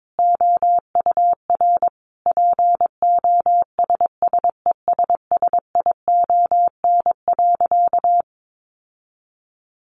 Morse code, heard as a keyed tone, about 22 wpm, 700 Hz. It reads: OUR POSSESSION.
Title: Sentences from Top 1400 Words 22wpm